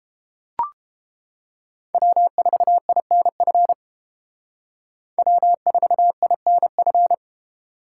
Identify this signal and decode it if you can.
Morse code, heard as a keyed tone, about 33 wpm, 700 Hz. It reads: W4INF W4INF